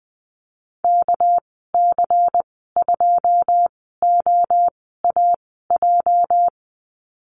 Morse code, heard as a keyed tone, about 20 wpm, spaced as for 14 wpm, 700 Hz. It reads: KC2OAJ
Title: 238 Lesson 38g - Letter q - Call Signs 20wpm